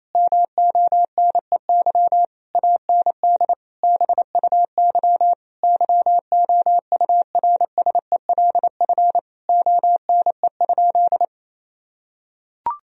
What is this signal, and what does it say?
Morse code, heard as a keyed tone, about 28 wpm, 700 Hz. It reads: MONEY AND BUY YOURSELF ONE?
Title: Sentences from Top 1000 Words - Review Part 4 - 28wpm